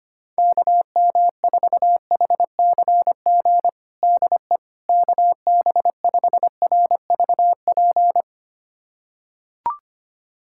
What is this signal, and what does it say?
Morse code, heard as a keyed tone, about 25 wpm, 700 Hz. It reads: KM4HCG DE KB5RVP
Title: USA Callsign DE Callsign 25wpm